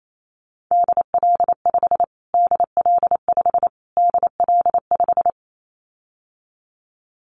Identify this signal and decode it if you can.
Morse code, heard as a keyed tone, about 28 wpm, 700 Hz. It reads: DL5 DL5 DL5
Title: Callsign Prefixes for Top 25 DXCC - Repeated 3x - 28wpm